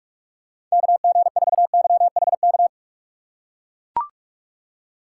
Morse code, heard as a keyed tone, about 44 wpm, 700 Hz. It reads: KG4YHK
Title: USA Call Signs - 6 Characters - 6x Character Spacing - 45wpm